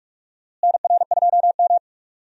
Morse code, heard as a keyed tone, about 46 wpm, 700 Hz. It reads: NR2M
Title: USA Call Signs - 4 Characters - Repeated 3x - 45wpm